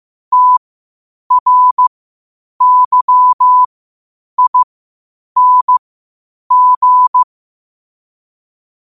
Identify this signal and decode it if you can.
Morse code, heard as a keyed tone, about 15 wpm, spaced as for 8 wpm, 1000 Hz. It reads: TRYING
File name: qotdfiles / 8wpm